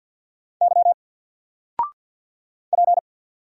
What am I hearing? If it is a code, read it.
Morse code, heard as a keyed tone, about 48 wpm, 700 Hz. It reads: Y P
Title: Single Letter-Number 50wpm